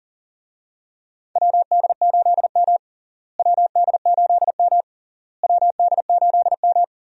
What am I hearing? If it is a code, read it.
Morse code, heard as a keyed tone, about 40 wpm, 700 Hz. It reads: WD8M WD8M WD8M
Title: USA Call Signs - 4 Characters - Repeated 3x - 40wpm